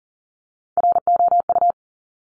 Morse code, heard as a keyed tone, about 39 wpm, 700 Hz. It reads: ROU
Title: Top Three Letter Combinations in Words 40wpm